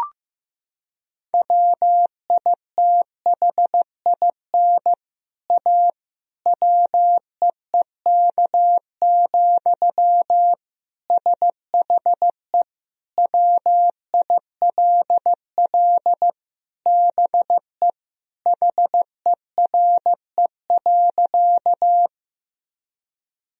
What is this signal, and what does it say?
Morse code, heard as a keyed tone, about 15 wpm, 700 Hz. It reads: WITHIN A WEEK, SHE WILL BE HERE.
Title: Sentences from Top 1000 Words - Review Part 4 - 15wpm